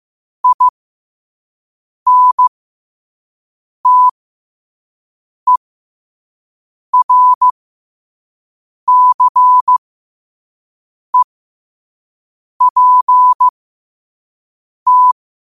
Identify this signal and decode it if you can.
Morse code, heard as a keyed tone, about 15 wpm, spaced as for 5 wpm, 1000 Hz. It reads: INTERCEPT
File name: qotdfiles / 5wpm